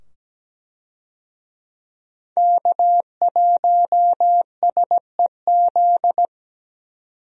Morse code, heard as a keyed tone, about 17 wpm, 700 Hz. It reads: K1SEZ
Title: USA Call Signs - 5 Characters - 17wpm